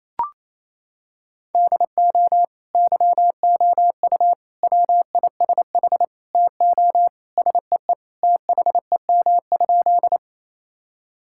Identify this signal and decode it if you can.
Morse code, heard as a keyed tone, about 28 wpm, 700 Hz. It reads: DO YOU WISH TO SEE THEM?
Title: Sentences from Top 1600 Words 28wpm